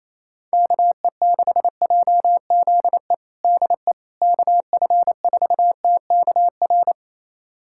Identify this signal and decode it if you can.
Morse code, heard as a keyed tone, about 28 wpm, 700 Hz. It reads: KE6JZE DE KF4TKR
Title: USA Callsign DE Callsign - 6 Characters - 28wpm